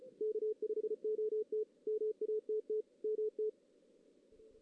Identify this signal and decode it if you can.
Morse code, heard as a keyed tone, about 32 wpm, 440 Hz. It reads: K5OT MATT MT
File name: 2016 JAN CW NAQP / 2016 Jan naqp k5ot - 0-2